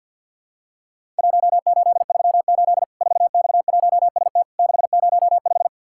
Morse code, heard as a keyed tone, about 48 wpm, 700 Hz. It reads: 1838 4X1ST 605
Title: Worked All Europe DX Contest - QTC Exchange - 50wpm